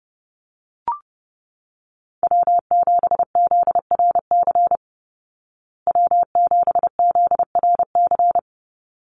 Morse code, heard as a keyed tone, about 30 wpm, 700 Hz. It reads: W7ZRC W7ZRC
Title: Call Signs Repeated 3x - 30wpm